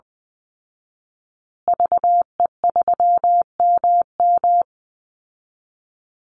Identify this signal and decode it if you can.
Morse code, heard as a keyed tone, about 20 wpm, 700 Hz. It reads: VE3MM
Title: Call Signs Repeated 2x - 20wpm